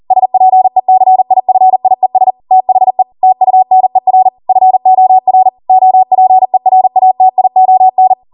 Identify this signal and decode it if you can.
Morse code, heard as a keyed tone, about 40 wpm, 750 Hz. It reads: SPECIFIES THE TUNER FOR OPERATION